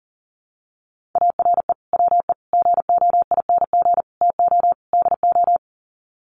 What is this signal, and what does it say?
Morse code, heard as a keyed tone, about 37 wpm, 700 Hz. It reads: ARE WE GOING TO DO
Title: Sets of 5 Words 40wpm